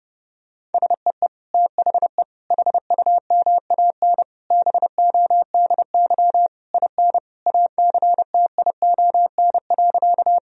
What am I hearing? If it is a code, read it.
Morse code, heard as a keyed tone, about 29 wpm, 700 Hz. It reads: SEE THE HUMAN BODY IN ACTION.
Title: Sentences from Top 1000 Words - Review Part 4 - 30wpm